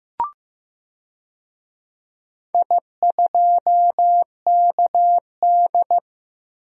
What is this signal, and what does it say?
Morse code, heard as a keyed tone, about 15 wpm, 700 Hz. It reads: I2KD
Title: DXCC-100 Call Signs - 4 Characters - Repeated 2x - 15wpm